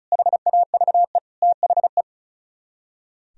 Morse code, heard as a keyed tone, about 45 wpm, 700 Hz. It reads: HAVE THE